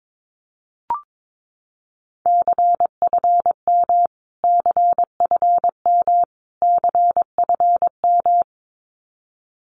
Morse code, heard as a keyed tone, about 22 wpm, 700 Hz. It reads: CFM CFM CFM
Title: Abbreviations Repeated 3x - 22wpm